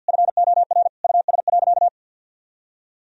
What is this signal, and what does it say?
Morse code, heard as a keyed tone, about 50 wpm, 700 Hz. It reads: FOR US.
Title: Sentences from Top 1000 Words - Review Part 3 - 50wpm